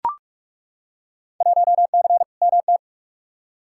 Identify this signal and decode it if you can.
Morse code, heard as a keyed tone, about 41 wpm, 700 Hz. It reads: 1C MT
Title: Field Day Exchanges Encore - 45wpm